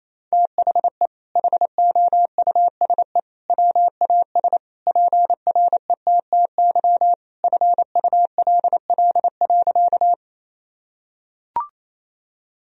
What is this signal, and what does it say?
Morse code, heard as a keyed tone, about 28 wpm, 700 Hz. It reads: THE HOUSE WAS PRETTY FULL.
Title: Sentences from Top 1000 Words - Review Part 3 - 28wpm